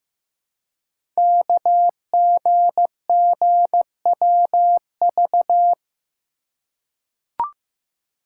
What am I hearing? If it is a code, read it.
Morse code, heard as a keyed tone, about 15 wpm, 700 Hz. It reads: KGGWV
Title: Code Groups - 5 Letters - 15wpm